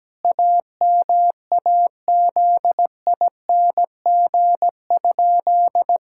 Morse code, heard as a keyed tone, about 17 wpm, 700 Hz. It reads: AMAZING?
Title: Pangram Challenge v2 - 17wpm